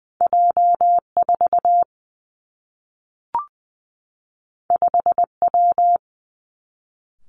Morse code, heard as a keyed tone, about 19 wpm, 700 Hz. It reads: J 4 5 W
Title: Two Challenging Letter-Number Combinations 20wpm